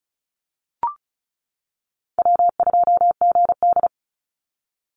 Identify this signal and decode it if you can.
Morse code, heard as a keyed tone, about 34 wpm, 700 Hz. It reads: W2GD
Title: Call Signs - 35wpm